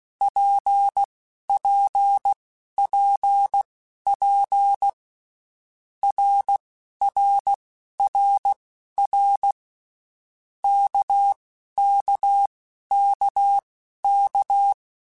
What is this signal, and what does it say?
Morse code, heard as a keyed tone, about 16 wpm, spaced as for 11 wpm, 800 Hz. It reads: PPPP RRRR KKKK
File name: Basic Letter Groups By7Groups / RKPX / RKPX@10